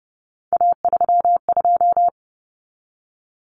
Morse code, heard as a keyed tone, about 30 wpm, 700 Hz. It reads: A32
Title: Common Cut Numbers - 3-Digit 30wpm